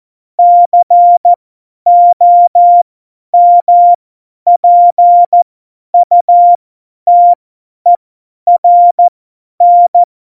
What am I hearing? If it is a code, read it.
Morse code, heard as a keyed tone, about 14 wpm, spaced as for 10 wpm, 700 Hz. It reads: COMPUTERN
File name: Morsekurs / Witze / Witz 23. April 2019, 50 F70